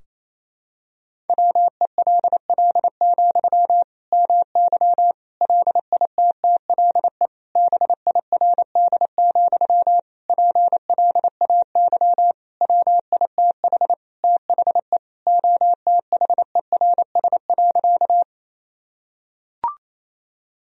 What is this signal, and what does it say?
Morse code, heard as a keyed tone, about 28 wpm, 700 Hz. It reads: WELL, MY LITTLE BIRD, PLAY WITH THE OTHERS.
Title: Sentences from Top 1000 Words - Review Part 2 - 28wpm